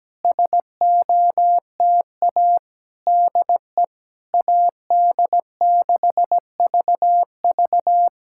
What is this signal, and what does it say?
Morse code, heard as a keyed tone, about 17 wpm, 700 Hz. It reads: SOTA DE AD6VV
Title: SOTA QSO Practice 17wpm